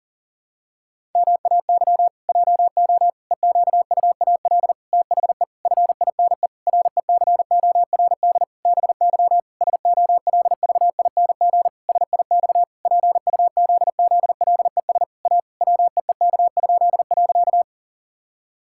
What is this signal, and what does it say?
Morse code, heard as a keyed tone, about 40 wpm, 700 Hz. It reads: MAY JO EQUAL THE FINE RECORD BY SOLVING SIX PUZZLES A WEEK?.
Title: Pangram Challenge v2 - 40wpm